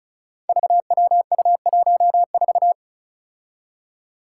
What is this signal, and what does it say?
Morse code, heard as a keyed tone, about 36 wpm, 700 Hz. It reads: VWU14
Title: Five Challenging Letter-Number Combinations 35wpm